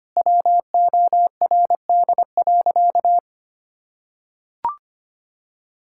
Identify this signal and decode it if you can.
Morse code, heard as a keyed tone, about 25 wpm, 700 Hz. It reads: WORD.
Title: Sentences from Top 1000 Words - Review Part 6 - 25wpm